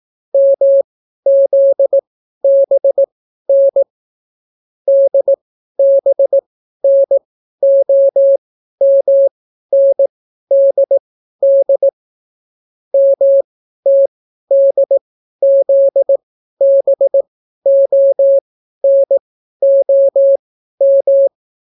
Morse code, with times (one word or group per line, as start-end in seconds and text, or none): 0.3-3.8 MZBN
4.9-11.9 DBNOMNDD
12.9-21.3 MTDZBONOM